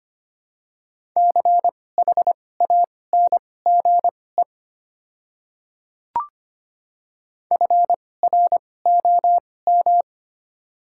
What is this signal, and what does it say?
Morse code, heard as a keyed tone, about 25 wpm, spaced as for 18 wpm, 690 Hz. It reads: CHANGE FROM
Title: 2x Character Spacing - Top 200 Words - 25wpm